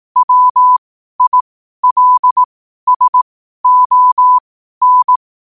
Morse code, heard as a keyed tone, about 18 wpm, spaced as for 12 wpm, 1000 Hz. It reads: WILSON
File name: qotdfiles / 12wpm